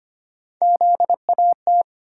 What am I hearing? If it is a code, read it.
Morse code, heard as a keyed tone, about 25 wpm, 700 Hz. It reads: ZAT
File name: Three Letter-Number Combinations - 25wpm